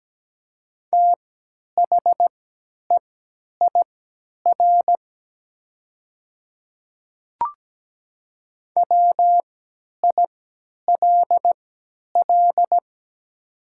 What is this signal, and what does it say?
Morse code, heard as a keyed tone, about 17 wpm, spaced as for 10 wpm, 700 Hz. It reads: THEIR WILL
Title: 3x Character Spacing - Top 100 Words - 17wpm